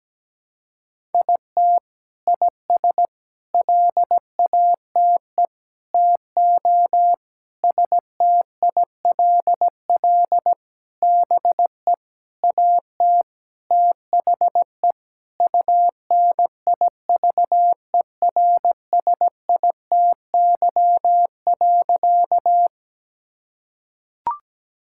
Morse code, heard as a keyed tone, about 17 wpm, 700 Hz. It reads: IT IS LATE TO STILL BE AT THE UNIVERSITY.
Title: Sentences from Top 1000 Words - Review Part 5 - 17wpm